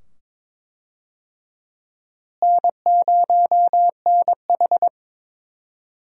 Morse code, heard as a keyed tone, about 21 wpm, 700 Hz. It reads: N0NH
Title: USA Call Signs - 4 Characters - Repeated 2x - 22wpm